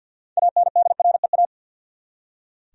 Morse code, heard as a keyed tone, about 45 wpm, 700 Hz. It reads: ANDREA
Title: USA Common Names - 50wpm